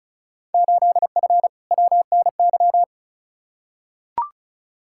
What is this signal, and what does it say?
Morse code, heard as a keyed tone, about 36 wpm, 700 Hz. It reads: 8F WNY
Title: Field Day Exchanges Encore - 35wpm